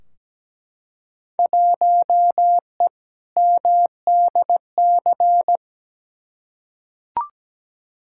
Morse code, heard as a keyed tone, about 17 wpm, 700 Hz. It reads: 1E MDC
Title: Field Day Exchanges - 17wpm